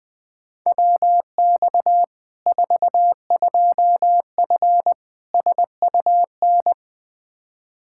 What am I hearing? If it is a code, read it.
Morse code, heard as a keyed tone, about 20 wpm, 700 Hz. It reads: WX 42F SUN
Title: K1USN SST QSO Practice 20wpm